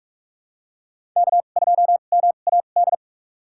Morse code, heard as a keyed tone, about 44 wpm, spaced as for 32 wpm, 700 Hz. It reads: K2MAD
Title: USA Call Signs - 2x Character Spacing - 45wpm